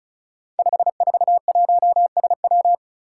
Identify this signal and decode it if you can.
Morse code, heard as a keyed tone, about 36 wpm, 700 Hz. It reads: 541SW
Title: Five Challenging Letter-Number Combinations 35wpm